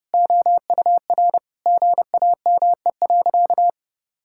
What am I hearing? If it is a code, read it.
Morse code, heard as a keyed tone, about 30 wpm, 700 Hz. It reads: OUR GAME.
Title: Sentences from Top 1000 Words - Review Part 5 - 30wpm